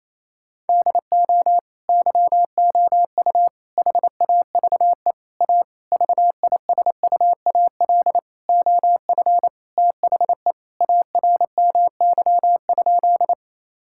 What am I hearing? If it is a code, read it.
Morse code, heard as a keyed tone, about 28 wpm, 700 Hz. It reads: DO YOU HAVE A VISUAL OF THE ARMY?
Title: Sentences from Top 1000 Words - Review Part 6 - 28wpm